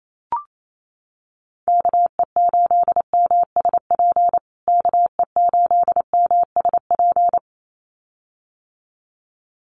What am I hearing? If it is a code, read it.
Morse code, heard as a keyed tone, about 28 wpm, 700 Hz. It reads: KE8MSP KE8MSP
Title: USA Call Signs Repeated 2x 28wpm